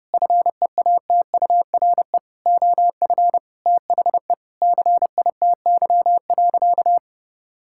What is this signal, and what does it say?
Morse code, heard as a keyed tone, about 30 wpm, 700 Hz. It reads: FEATURE OF THE CITY.
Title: Sentences from Top 1000 Words - Review Part 5 - 30wpm